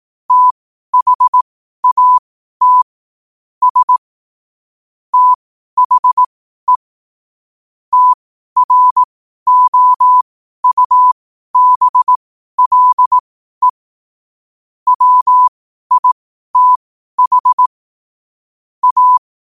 Morse code, heard as a keyed tone, about 18 wpm, spaced as for 12 wpm, 1000 Hz. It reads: THAT S THE TROUBLE WITH A
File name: qotdfiles / 12wpm